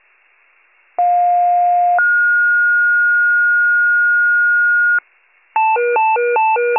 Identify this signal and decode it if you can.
Two-tone page, heard as a one-second tone, then a three-second tone, about 710 then 1430 Hz. It is heard from 1.0 to 5.0 s.